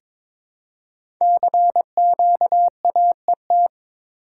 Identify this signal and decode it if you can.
Morse code, heard as a keyed tone, about 22 wpm, 700 Hz. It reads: CQAET
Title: Code Groups - 5 Letter:Numbers - 22wpm